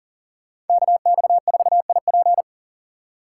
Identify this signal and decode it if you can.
Morse code, heard as a keyed tone, about 52 wpm, 700 Hz. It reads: KX4IP